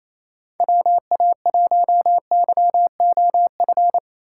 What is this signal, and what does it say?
Morse code, heard as a keyed tone, about 28 wpm, 700 Hz. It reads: WA1YOF
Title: USA Call Signs - 6 Characters - 3x Character Spacing - 28wpm